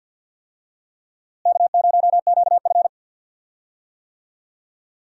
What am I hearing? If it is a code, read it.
Morse code, heard as a keyed tone, about 46 wpm, 700 Hz. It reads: K0QF
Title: USA Call Signs - 4 Characters - Repeated 2x - 50wpm